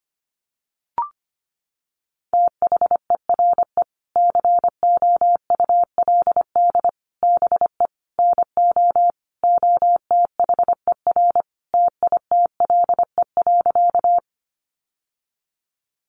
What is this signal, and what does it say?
Morse code, heard as a keyed tone, about 25 wpm, 700 Hz. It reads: THERE COULD BE NO OTHER TITLE.
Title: Sentences from Top 1200 Words 25wpm